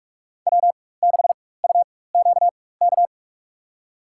Morse code, heard as a keyed tone, about 46 wpm, spaced as for 21 wpm, 700 Hz. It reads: W6UQK
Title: USA Call Signs - 4x Character Spacing - 45wpm